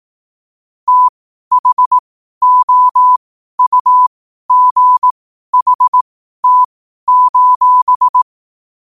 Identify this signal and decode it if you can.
Morse code, heard as a keyed tone, about 18 wpm, spaced as for 12 wpm, 1000 Hz. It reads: THOUGHT:
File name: qotdfiles / 12wpm